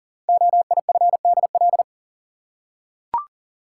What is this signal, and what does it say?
Morse code, heard as a keyed tone, about 39 wpm, 700 Hz. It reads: OIFDL
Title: Code Groups - 5 Letters - 40wpm